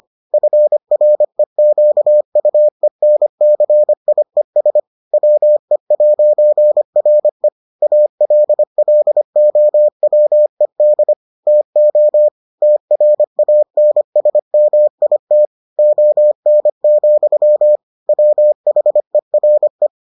Morse code, heard as a keyed tone, about 25 wpm, 600 Hz. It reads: FREQUENCIES WE'RE ALLOWED TO TRANSMIT ON, WHERE